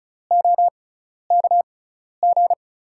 Morse code, heard as a keyed tone, about 36 wpm, spaced as for 12 wpm, 700 Hz. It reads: OKG
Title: USA Call Signs - 6x Character Spacing - 35wpm